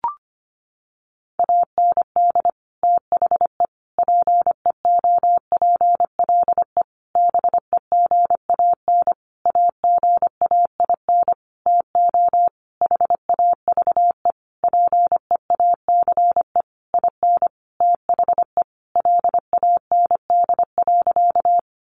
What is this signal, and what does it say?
Morse code, heard as a keyed tone, about 25 wpm, 700 Hz. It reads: AND THE PEOPLE BEGAN AGAIN TO HAVE PEACE IN THE LAND.
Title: Sentences from Top 1500 Words 25wpm